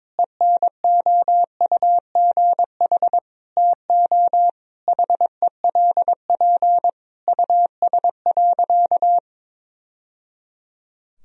Morse code, heard as a keyed tone, about 22 wpm, 700 Hz. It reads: ENOUGH TO HELP US.
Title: Sentences from Top 1000 Words - Review Part 3 - 22wpm